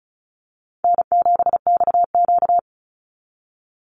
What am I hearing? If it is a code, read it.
Morse code, heard as a keyed tone, about 34 wpm, 700 Hz. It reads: N7XQ
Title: USA Call Signs 35wpm